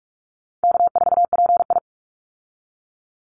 Morse code, heard as a keyed tone, about 44 wpm, 700 Hz. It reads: K4PI
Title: Easy Call Signs Repeated 2x - 45wpm